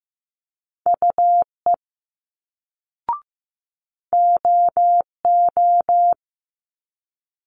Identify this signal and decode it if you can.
Morse code, heard as a keyed tone, about 15 wpm, 700 Hz. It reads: UE OO
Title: Multi-letter Phonemes Quick Fire 15wpm